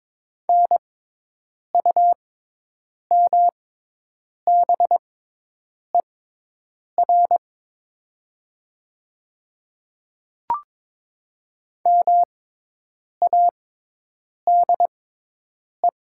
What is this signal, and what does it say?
Morse code, heard as a keyed tone, about 22 wpm, spaced as for 8 wpm, 700 Hz. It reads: NUMBER MADE
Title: 6x Character Spacing - Top 200 Words - 22wpm